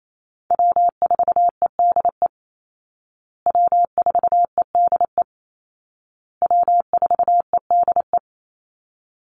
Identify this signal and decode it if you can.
Morse code, heard as a keyed tone, about 28 wpm, 700 Hz. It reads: W4EDE W4EDE W4EDE
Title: Call Signs Repeated 2x - 28wpm